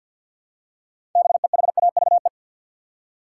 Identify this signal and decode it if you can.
Morse code, heard as a keyed tone, about 49 wpm, 700 Hz. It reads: BEHAVE
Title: Six Letter Words - 50wpm